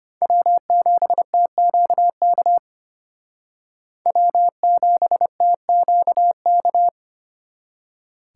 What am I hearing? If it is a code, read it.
Morse code, heard as a keyed tone, about 33 wpm, 700 Hz. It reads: W7TQK W7TQK